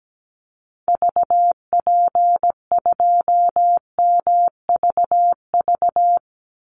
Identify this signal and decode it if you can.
Morse code, heard as a keyed tone, about 17 wpm, 700 Hz. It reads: VP2MVV
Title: Call Signs Repeated 2x - 17wpm